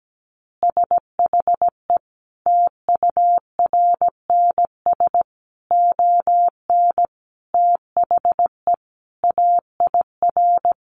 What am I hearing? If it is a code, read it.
Morse code, heard as a keyed tone, about 17 wpm, 700 Hz. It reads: SHE TURNS ON THE AIR
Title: Easy Phrases 17wpm